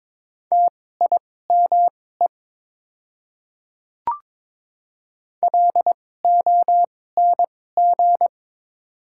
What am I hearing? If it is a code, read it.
Morse code, heard as a keyed tone, about 22 wpm, spaced as for 16 wpm, 700 Hz. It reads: TIME LONG
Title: 2x Character Spacing - Top 200 Words - 22wpm